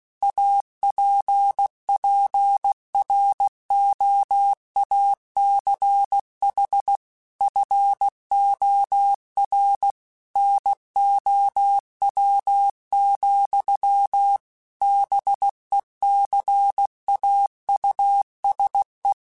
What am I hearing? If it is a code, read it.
Morse code, heard as a keyed tone, about 16 wpm, 800 Hz. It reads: APPROACH FOR NOW, BECAUSE
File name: Technical Subjects / Technical Subjects-mp3s / Gates Intro @15